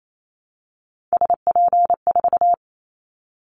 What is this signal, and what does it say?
Morse code, heard as a keyed tone, about 28 wpm, 700 Hz. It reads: SP4
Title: Callsign Prefixes for Top 25 DXCC - 28wpm